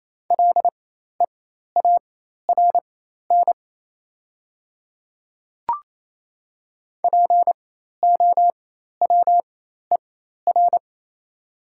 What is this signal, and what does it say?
Morse code, heard as a keyed tone, about 28 wpm, spaced as for 13 wpm, 700 Hz. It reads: LEARN POWER
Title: 4x Character Spacing - Top 500 Words - 28wpm